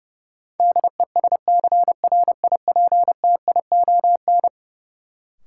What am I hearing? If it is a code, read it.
Morse code, heard as a keyed tone, about 29 wpm, 700 Hz. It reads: DESCRIPTION
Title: Sentences from Top 1300 Words - New Word Drill - 30wpm